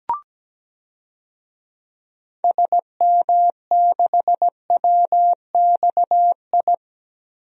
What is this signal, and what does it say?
Morse code, heard as a keyed tone, about 17 wpm, 700 Hz. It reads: SM6WXI
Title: DXCC-100 Call Signs - 6 Characters - Repeated 2x - 17wpm